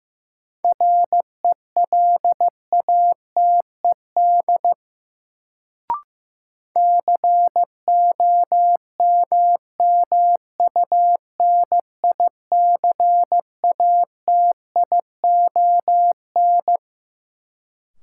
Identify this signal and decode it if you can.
Morse code, heard as a keyed tone, about 15 wpm, 700 Hz. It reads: RELATED COMMUNICATION
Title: Sentences from Top 1400 Words - New Word Drill Alt - 15wpm